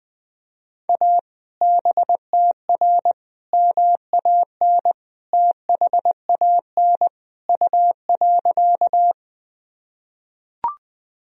Morse code, heard as a keyed tone, about 20 wpm, 700 Hz. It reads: A BTR MAN THAN U.
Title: Sentences from Top 400 Words and Abbreviations 20wpm